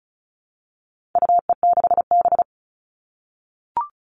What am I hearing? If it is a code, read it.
Morse code, heard as a keyed tone, about 34 wpm, 700 Hz. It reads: UE6B
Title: All Cut Numbers - 4-Digit 35wpm